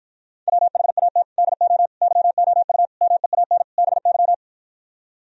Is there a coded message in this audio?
Yes: Morse code from 0.5 to 4.3 s, heard as a keyed tone, about 53 wpm, 700 Hz. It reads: WHAT DO YOU MEAN BY